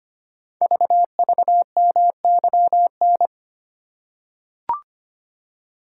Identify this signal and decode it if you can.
Morse code, heard as a keyed tone, about 25 wpm, 700 Hz. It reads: VVMYN
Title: Code Groups - 5 Letters - 25wpm